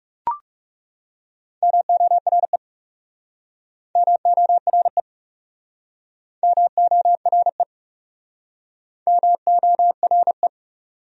Morse code, heard as a keyed tone, about 40 wpm, 700 Hz. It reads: MORE MORE MORE MORE